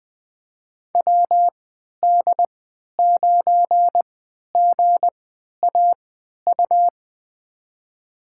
Morse code, heard as a keyed tone, about 21 wpm, spaced as for 11 wpm, 700 Hz. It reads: WD9GAU
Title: USA Call Signs - 6 Characters - 3x Character Spacing - 20wpm